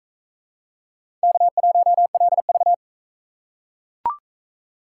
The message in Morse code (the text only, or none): K1LV